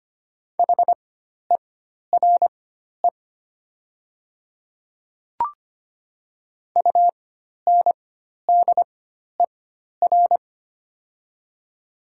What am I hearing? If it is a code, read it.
Morse code, heard as a keyed tone, about 25 wpm, spaced as for 12 wpm, 700 Hz. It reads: HERE UNDER
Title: 4x Character Spacing - Top 200 Words - 25wpm